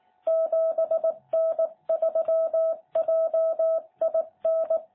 Morse code, heard as a keyed tone, about 18 wpm, 660 Hz. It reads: 7N3JIN
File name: jh8cbh / 2021cw